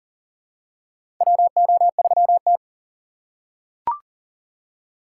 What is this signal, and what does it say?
Morse code, heard as a keyed tone, about 40 wpm, 700 Hz. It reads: WO3T
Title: USA Call Signs - 4 Characters - 3x Character Spacing - 40wpm